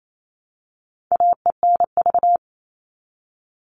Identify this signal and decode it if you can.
Morse code, heard as a keyed tone, about 28 wpm, 700 Hz. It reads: AENV
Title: All Cut Numbers - 4-Digit 28wpm